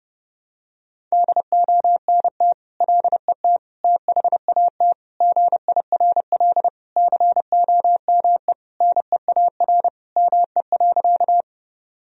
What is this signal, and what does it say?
Morse code, heard as a keyed tone, about 30 wpm, 700 Hz. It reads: DONT LET THAT GIRL COME NEAR ME.
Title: Sentences from Top 1000 Words - Review Part 1 - 30wpm